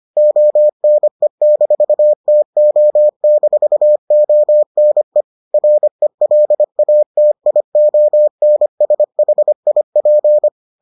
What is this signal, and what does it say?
Morse code, heard as a keyed tone, about 25 wpm, 600 Hz. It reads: ONE-TO-ONE RELATIONSHIP